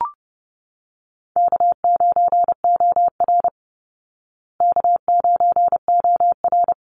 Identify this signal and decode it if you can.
Morse code, heard as a keyed tone, about 30 wpm, 700 Hz. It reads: K9OR K9OR
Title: Easy Call Signs Repeated 2x - 30wpm